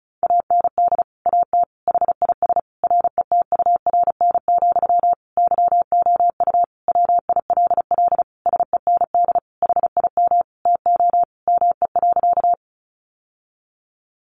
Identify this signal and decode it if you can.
Morse code, heard as a keyed tone, about 34 wpm, 700 Hz. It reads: AND AT HIS RETURN, YOU WILL SEND HIM TO ME.
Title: Sentences from Top 1500 Words 35wpm